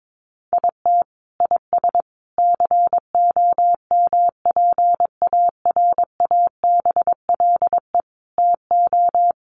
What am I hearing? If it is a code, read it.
Morse code, heard as a keyed tone, about 22 wpm, 700 Hz. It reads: IT IS COMPARABLE TO
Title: A Sound Of Thunder 22wpm